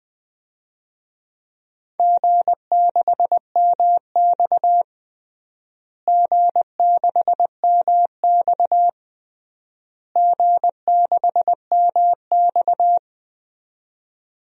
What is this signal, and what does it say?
Morse code, heard as a keyed tone, about 20 wpm, 700 Hz. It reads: G6MX G6MX G6MX
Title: DXCC-100 Call Signs - 4 Characters - Repeated 3x - 20wpm